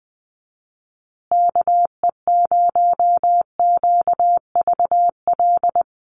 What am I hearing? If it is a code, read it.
Morse code, heard as a keyed tone, about 20 wpm, 700 Hz. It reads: KE0QVL
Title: USA Call Signs Repeated 2x 20wpm